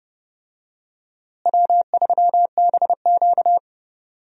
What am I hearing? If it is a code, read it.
Morse code, heard as a keyed tone, about 32 wpm, 700 Hz. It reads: W3BQ
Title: USA Call Signs - 4 Characters - Repeated 2x - 30wpm